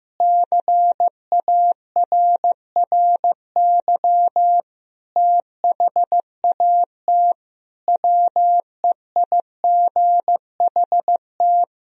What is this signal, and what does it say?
Morse code, heard as a keyed tone, about 14 wpm, 700 Hz. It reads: CARRY THAT WEIGHT
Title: Sentences from Top 1000 Words - Review Part 3 - 15wpm